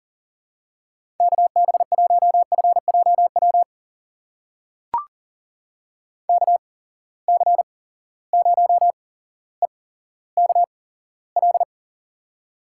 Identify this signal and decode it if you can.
Morse code, heard as a keyed tone, about 38 wpm, spaced as for 17 wpm, 700 Hz. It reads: KB1FJW KC0EKL
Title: USA Call Signs - 6 Characters - 8x Character Spacing - 40wpm